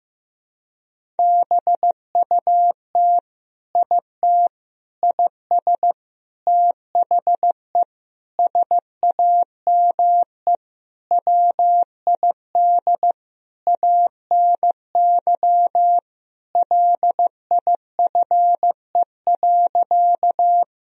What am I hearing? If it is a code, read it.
Morse code, heard as a keyed tone, about 16 wpm, 700 Hz. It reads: BUT IT IS THE SAME WID ANY LIFE.
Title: Sentences from Top 300 Words and Abbreviations 15wpm